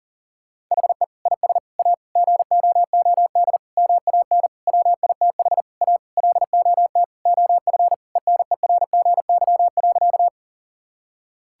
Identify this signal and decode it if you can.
Morse code, heard as a keyed tone, about 40 wpm, 700 Hz. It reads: HE IS A GOOD MAN WITH A LOT OF ENERGY.
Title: Sentences from Top 1000 Words - Review Part 6 - 40wpm